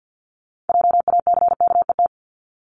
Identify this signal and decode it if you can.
Morse code, heard as a keyed tone, about 50 wpm, 700 Hz. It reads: JACKET
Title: Multi-letter Phonemes 50wpm